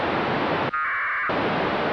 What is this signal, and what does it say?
Packet radio (AX.25, as in APRS).
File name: satellite / raft / raft psk31 pe1itr 28jan2007 0622UTC